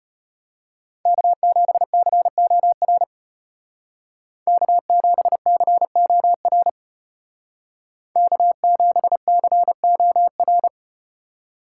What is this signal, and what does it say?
Morse code, heard as a keyed tone, about 35 wpm, 700 Hz. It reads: K7COR K7COR K7COR